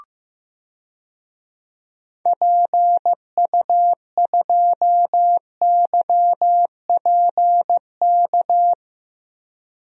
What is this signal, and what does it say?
Morse code, heard as a keyed tone, about 16 wpm, 700 Hz. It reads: PU2YPK
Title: DXCC-100 Call Signs - 6 Characters - Repeated 2x - 15wpm